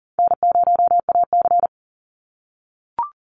Morse code, heard as a keyed tone, about 40 wpm, 700 Hz. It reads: N0AC
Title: Easy Call Signs - 40wpm